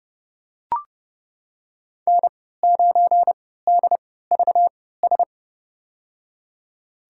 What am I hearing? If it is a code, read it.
Morse code, heard as a keyed tone, about 30 wpm, spaced as for 17 wpm, 700 Hz. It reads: N9DVS
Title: USA Call Signs - 5 Characters - 3x Character Spacing - 30wpm